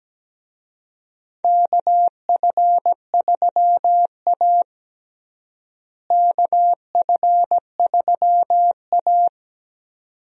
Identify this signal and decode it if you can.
Morse code, heard as a keyed tone, about 17 wpm, 700 Hz. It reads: KF3A KF3A
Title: USA Call Signs - 4 Characters - Repeated 2x - 17wpm